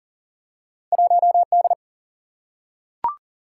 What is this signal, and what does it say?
Morse code, heard as a keyed tone, about 39 wpm, 700 Hz. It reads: 1D
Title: Single-Letter Single-Number 40wpm